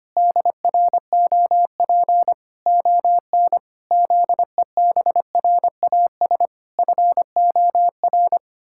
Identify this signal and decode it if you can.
Morse code, heard as a keyed tone, about 25 wpm, 700 Hz. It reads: DROP ON ZEBRAS FOR
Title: Pangram Challenge v2 - 25wpm